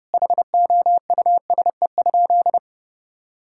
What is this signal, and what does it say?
Morse code, heard as a keyed tone, about 30 wpm, 700 Hz. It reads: HOUSE?
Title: Sentences from Top 1000 Words - Review Part 5 - 30wpm